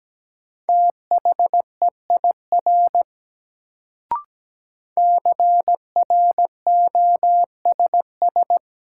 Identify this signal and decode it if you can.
Morse code, heard as a keyed tone, about 17 wpm, 700 Hz. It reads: THEIR CROSS
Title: Sentences from Top 300 Words - New Word Drill - 17wpm